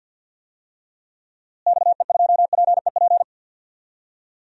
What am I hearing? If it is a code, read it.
Morse code, heard as a keyed tone, about 48 wpm, 700 Hz. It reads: XE2PEP
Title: DXCC-100 Call Signs - 6 Characters - Repeated 2x - 50wpm